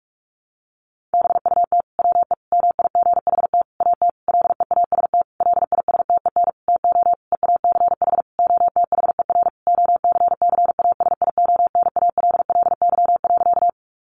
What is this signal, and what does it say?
Morse code, heard as a keyed tone, about 44 wpm, 700 Hz. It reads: BUT WE MIGHT AT LEAST LISTEN TO EACH OTHER OCCASIONALLY.
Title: Sentences from Top 1300 Words 45wpm